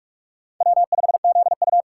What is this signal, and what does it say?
Morse code, heard as a keyed tone, about 44 wpm, 700 Hz. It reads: W5ZU